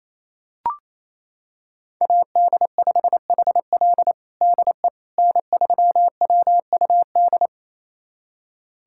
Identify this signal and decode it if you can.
Morse code, heard as a keyed tone, about 28 wpm, 700 Hz. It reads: AD5HL DE N3WUD
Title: USA Callsign DE Callsign - 5 Characters - 28wpm